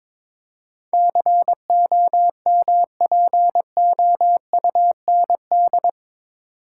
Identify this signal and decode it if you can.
Morse code, heard as a keyed tone, about 22 wpm, 700 Hz. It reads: COMPOUND
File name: practice / 22wpm / Eight Letter Words - 22wpm